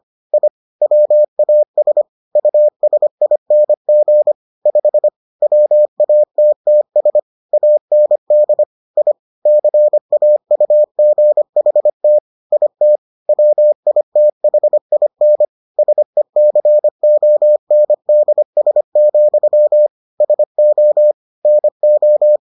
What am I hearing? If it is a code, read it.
Morse code, heard as a keyed tone, about 25 wpm, 600 Hz. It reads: I WAS USING 5 WATTS AND I CAUGHT IT WITHIN SECONDS, SO NO